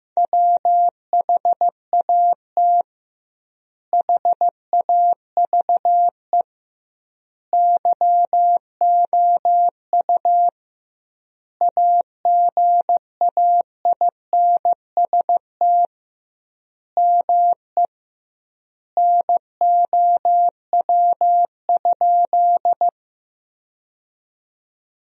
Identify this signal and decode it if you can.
Morse code, heard as a keyed tone, about 15 wpm, 700 Hz. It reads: WHAT HAVE YOU AGAINST ME NOW?
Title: 2x Word-Spacing - Sentences from Top 400 Words - 15wpm